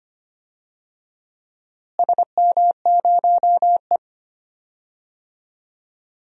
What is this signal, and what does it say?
Morse code, heard as a keyed tone, about 25 wpm, 700 Hz. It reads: SM0E
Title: DXCC-100 Call Signs - 4 Characters - Repeated 2x - 25wpm